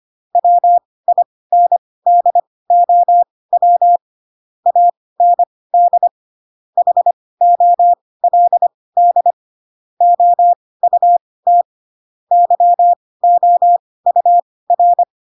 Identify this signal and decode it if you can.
Morse code, heard as a keyed tone, about 25 wpm, spaced as for 18 wpm, 700 Hz. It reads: WINDOW AND HOLD OUT YOUR
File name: Pinocchio / Pinocchio ch6 12 25 ik1hge